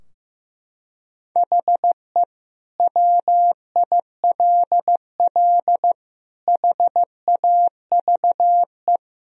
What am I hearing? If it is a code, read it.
Morse code, heard as a keyed tone, about 15 wpm, 700 Hz. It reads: HE WILL HAVE
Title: Sentences from Top 1000 Words - Review Part 3 - 15wpm